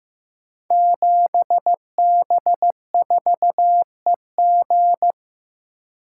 Morse code, heard as a keyed tone, about 15 wpm, 700 Hz. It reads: 7B4EG
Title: Code Groups - 5 Letter:Numbers - 15wpm